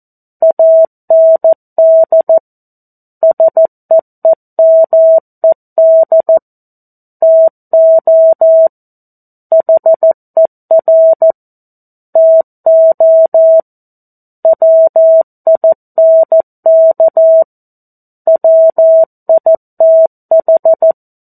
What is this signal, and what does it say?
Morse code, heard as a keyed tone, about 14 wpm, 650 Hz. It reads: AND SEEMED TO HER TO WINK WITH
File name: 02-Chapter-02 14 WPM